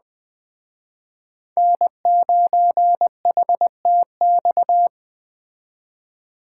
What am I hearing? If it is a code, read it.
Morse code, heard as a keyed tone, about 20 wpm, 700 Hz. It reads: N9HTX